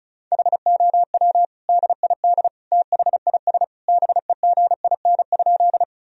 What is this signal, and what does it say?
Morse code, heard as a keyed tone, about 34 wpm, 700 Hz. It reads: HOW DID THIS BEGIN?
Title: Sentences from Top 1000 Words - Review Part 1 - 35wpm